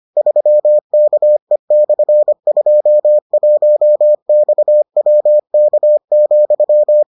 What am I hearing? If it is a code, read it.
Morse code, heard as a keyed tone, about 25 wpm, 600 Hz. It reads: 3KE/21XWK,